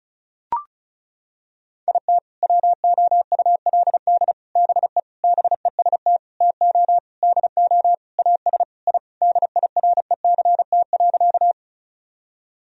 Morse code, heard as a keyed tone, about 36 wpm, 700 Hz. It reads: IT WOULD BE BEST TO DO AS I DIRECT.
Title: Sentences from Top 1000 Words - Review Part 2 - 35wpm